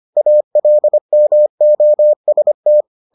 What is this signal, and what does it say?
Morse code, heard as a keyed tone, about 25 wpm, 600 Hz. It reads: ALMOST